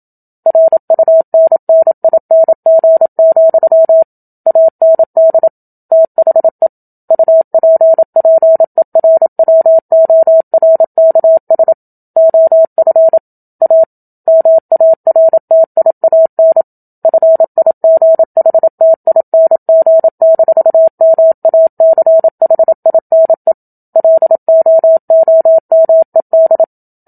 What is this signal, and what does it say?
Morse code, heard as a keyed tone, about 27 wpm, 650 Hz. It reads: RUNNING, AND THE UPPERWORKS OF A MARTIAN FIGHTING-MACHINE LOOMED
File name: Book Two / 01-Chapter 1 - 27 WPM